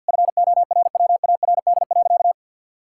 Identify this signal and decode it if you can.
Morse code, heard as a keyed tone, about 50 wpm, 700 Hz. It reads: FORWARD.
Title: Pangram Challenge v2 - 50wpm